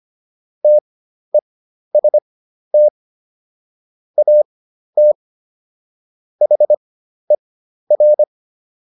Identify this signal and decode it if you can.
Morse code, heard as a keyed tone, about 25 wpm, spaced as for 12 wpm, 600 Hz. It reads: TEST AT HER